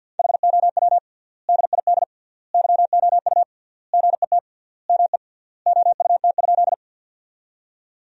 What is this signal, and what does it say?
Morse code, heard as a keyed tone, about 52 wpm, 700 Hz. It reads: HOW DID YOU GET ME OUT?
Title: 3x Word-Spacing - Sentences from Top 200 Words - 50wpm